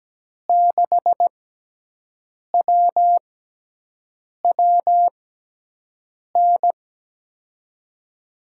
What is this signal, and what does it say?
Morse code, heard as a keyed tone, about 17 wpm, spaced as for 6 wpm, 700 Hz. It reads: 6WWN
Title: USA Call Signs - 6x Character Spacing - 17wpm